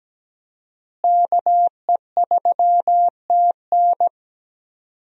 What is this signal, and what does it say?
Morse code, heard as a keyed tone, about 17 wpm, 700 Hz. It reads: KE3TN
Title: USA Call Signs - 5 Characters - Repeated 3x - 17wpm